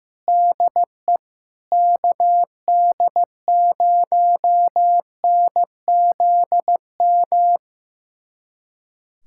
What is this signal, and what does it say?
Morse code, heard as a keyed tone, about 15 wpm, 700 Hz. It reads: DE KD0NZM
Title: USA Callsign DE Callsign - 6 Characters - 15wpm